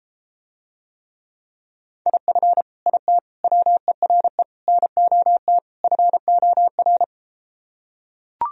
Morse code, heard as a keyed tone, about 33 wpm, 700 Hz. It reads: IF IT WERE NOT FOR